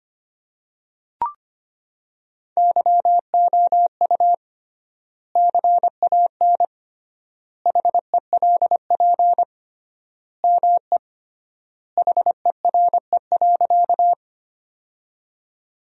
Morse code, heard as a keyed tone, about 25 wpm, 700 Hz. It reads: YOU CAN HELP ME HERE.
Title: 3x Word-Spacing - Sentences from Top 200 Words - 25wpm